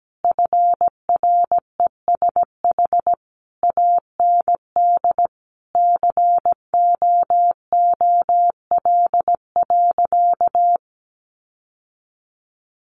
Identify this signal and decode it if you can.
Morse code, heard as a keyed tone, about 17 wpm, 700 Hz. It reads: FRESH AND COOL.
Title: Sentences from Top 1200 Words 17wpm